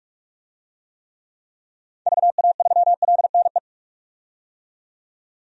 Morse code, heard as a keyed tone, about 46 wpm, 700 Hz. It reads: VA3LNE
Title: DXCC-100 Call Signs - 6 Characters - 45wpm